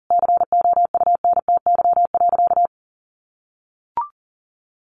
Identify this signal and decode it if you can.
Morse code, heard as a keyed tone, about 37 wpm, 700 Hz. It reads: COUNTY.
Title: Sentences from Top 1200 Words 40wpm